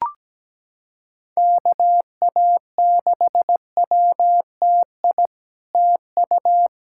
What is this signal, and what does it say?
Morse code, heard as a keyed tone, about 17 wpm, 700 Hz. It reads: KA6WTI TU
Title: SOTA QSO Practice 17wpm